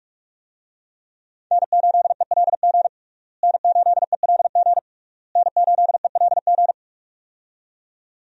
Morse code, heard as a keyed tone, about 46 wpm, 700 Hz. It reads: N8ELG N8ELG N8ELG
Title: USA Call Signs - 5 Characters - Repeated 3x - 45wpm